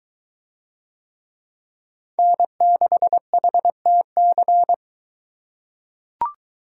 Morse code, heard as a keyed tone, about 21 wpm, 700 Hz. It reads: N6HTC